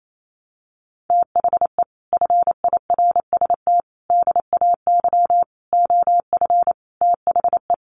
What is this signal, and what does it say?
Morse code, heard as a keyed tone, about 28 wpm, 700 Hz. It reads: THE FIRST DAY OF THE
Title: Sets of 5 Words Encore2 28wpm